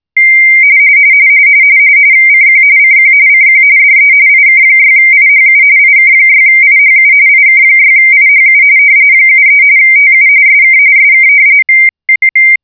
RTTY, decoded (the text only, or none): CQ CQ CQ DE IK6IHU IK6IHU IK6IHU 73 TU SK SK